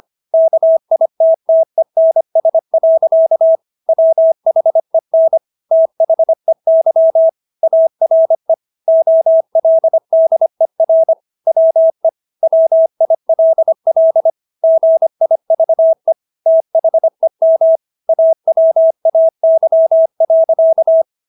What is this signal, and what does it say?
Morse code, heard as a keyed tone, about 25 wpm, 650 Hz. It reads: KITTENS. WHEN THEY ARE OLDER WE WILL GIVE THEM AWAY.